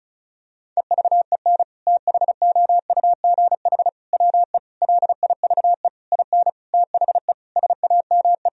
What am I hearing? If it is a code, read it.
Morse code, heard as a keyed tone, about 36 wpm, 700 Hz. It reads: EVEN THOUGH WE LIVE IN THE SAME
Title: Sentences from Top 1000 Words - Review Part 2 - 35wpm